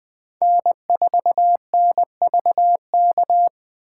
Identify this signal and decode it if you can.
Morse code, heard as a keyed tone, about 24 wpm, 700 Hz. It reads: N4NVK
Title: Speed Racing - USA 1x3 Call Signs 20wpm